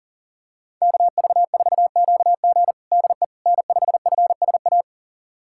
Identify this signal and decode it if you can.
Morse code, heard as a keyed tone, about 40 wpm, 700 Hz. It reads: KV4QG DE N5FSA
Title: USA Callsign DE Callsign - 5 Characters - 40wpm